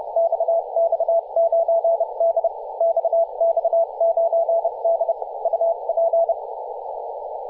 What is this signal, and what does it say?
Morse code, heard as a keyed tone, about 30 wpm, spaced as for 26 wpm, 670 Hz, with radio noise. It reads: XX9D XX9D UP
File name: posts / 2017-feb-16-0552z-xx9d-21-026mhz-cw